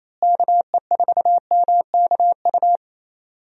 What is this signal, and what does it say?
Morse code, heard as a keyed tone, about 28 wpm, 700 Hz. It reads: KE4MKU
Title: USA Call Signs - 6 Characters - Repeated 3x - 28wpm